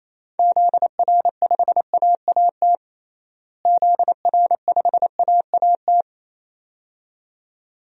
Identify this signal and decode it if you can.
Morse code, heard as a keyed tone, about 28 wpm, 700 Hz. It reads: ZR5AAT ZR5AAT
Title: DXCC-100 Call Signs - 6 Characters - Repeated 2x - 28wpm